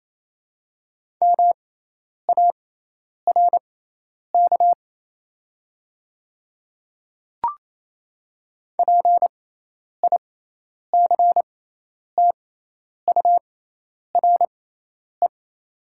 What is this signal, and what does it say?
Morse code, heard as a keyed tone, about 27 wpm, spaced as for 10 wpm, 700 Hz. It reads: MARK PICTURE
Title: 6x Character Spacing - Top 300 Words - 28wpm